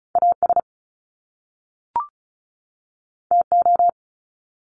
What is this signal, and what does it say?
Morse code, heard as a keyed tone, about 36 wpm, 700 Hz. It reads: AS TO
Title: Top Two Letter Combinations in Words 35wpm